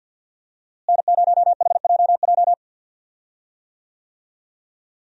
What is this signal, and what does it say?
Morse code, heard as a keyed tone, about 52 wpm, 700 Hz. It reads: N0HJJ
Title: USA Call Signs - 5 Characters - Repeated 2x - 50wpm